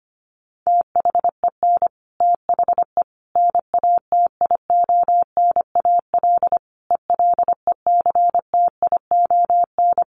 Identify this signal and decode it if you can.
Morse code, heard as a keyed tone, about 25 wpm, 700 Hz. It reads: THEN THE NATIONAL ELECTION
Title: Sentences from Top 1500 Words 25wpm